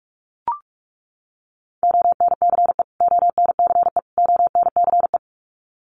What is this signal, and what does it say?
Morse code, heard as a keyed tone, about 44 wpm, 700 Hz. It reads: ONCE ONCE ONCE
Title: Top 500 Words Repeated 45wpm